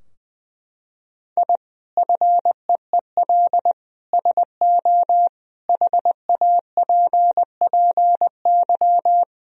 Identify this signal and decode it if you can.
Morse code, heard as a keyed tone, about 20 wpm, 700 Hz. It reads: I FEEL SO HAPPY
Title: Sentences from Top 1000 Words - Review Part 3 - 20wpm